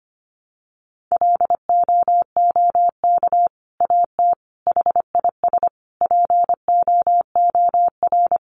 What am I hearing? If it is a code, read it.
Morse code, heard as a keyed tone, about 25 wpm, 700 Hz. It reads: LOOK AT HIS POOR
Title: Sentences from Top 1300 Words 25wpm